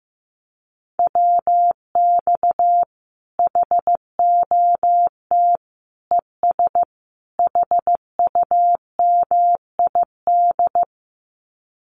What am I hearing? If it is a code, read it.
Morse code, heard as a keyed tone, about 14 wpm, 700 Hz. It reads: WX HOT ES HUMID
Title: Common QSO Phrases 15wpm